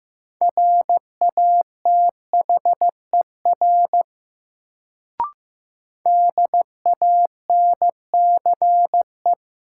Morse code, heard as a keyed tone, about 15 wpm, 700 Hz. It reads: RATHER DANCE
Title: Sentences from Top 700 Words - New Word Drill - 15wpm